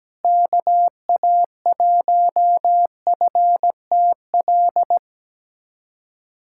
Morse code, heard as a keyed tone, about 17 wpm, 700 Hz. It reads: KA1FTL
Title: USA Callsign DE Callsign - 6 Characters - 17wpm